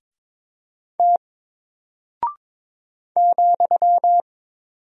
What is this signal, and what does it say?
Morse code, heard as a keyed tone, about 23 wpm, 700 Hz. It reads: T ,
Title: Single Character (Letter:Number:Punctuation:Prosign) Alt 22wpm